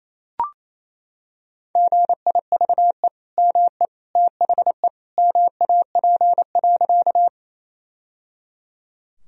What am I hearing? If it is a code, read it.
Morse code, heard as a keyed tone, about 27 wpm, 700 Hz. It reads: GIVE ME THE MAP.
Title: Sentences from Top 1000 Words - Review Part 2 - 28wpm